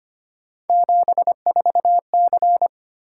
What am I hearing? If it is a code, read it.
Morse code, heard as a keyed tone, about 25 wpm, 700 Hz. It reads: 74C
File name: Three Letter-Number Combinations - 25wpm